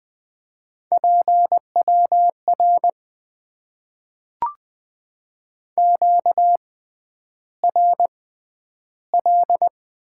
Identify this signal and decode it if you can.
Morse code, heard as a keyed tone, about 20 wpm, spaced as for 10 wpm, 700 Hz. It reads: PWR QRL
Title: Top 90 QSO Elements - 6x Character Spacing - 20wpm